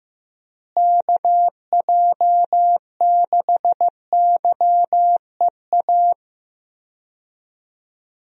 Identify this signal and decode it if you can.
Morse code, heard as a keyed tone, about 15 wpm, 700 Hz. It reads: KJ6YEA
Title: USA Call Signs - 6 Characters - Repeated 2x - 15wpm